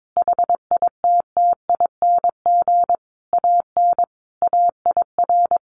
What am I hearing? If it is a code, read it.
Morse code, heard as a keyed tone, about 22 wpm, 700 Hz. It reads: HITTING AN AIR
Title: A Sound Of Thunder 22wpm